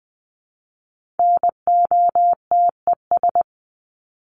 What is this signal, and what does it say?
Morse code, heard as a keyed tone, about 20 wpm, 700 Hz. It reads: NOTES
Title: ES:S Suffix - Top 100 Words with Base Words 20wpm